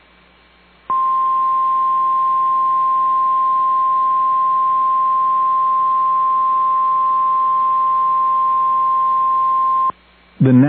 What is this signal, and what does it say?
Emergency Alert System (SAME) tones (the 1050 Hz attention tone), with speech. It is heard from 0.8 to 10.0 s.